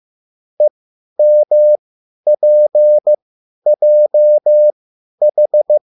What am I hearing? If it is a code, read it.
Morse code, heard as a keyed tone, about 15 wpm, spaced as for 10 wpm, 600 Hz. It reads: EMPJH
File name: QSL / CW lesson